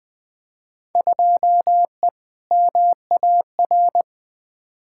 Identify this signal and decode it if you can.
Morse code, heard as a keyed tone, about 20 wpm, 700 Hz. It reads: 2E MAR
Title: Field Day Exchanges Encore - 20wpm